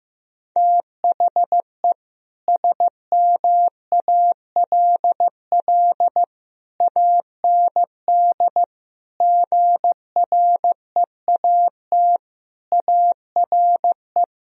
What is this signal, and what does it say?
Morse code, heard as a keyed tone, about 16 wpm, 700 Hz. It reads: THE SMALL AND GREAT ARE
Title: Sentences from Top 1000 Words - Review Part 1 - 15wpm